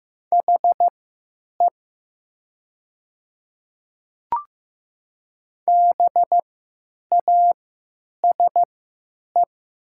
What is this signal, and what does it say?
Morse code, heard as a keyed tone, about 15 wpm, spaced as for 9 wpm, 700 Hz. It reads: HE BASE
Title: 3x Character Spacing - Top 300 Words - 15wpm